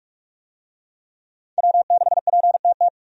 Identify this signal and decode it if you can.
Morse code, heard as a keyed tone, about 42 wpm, 700 Hz. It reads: W6PTT
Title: USA Call Signs - 5 Characters - 45wpm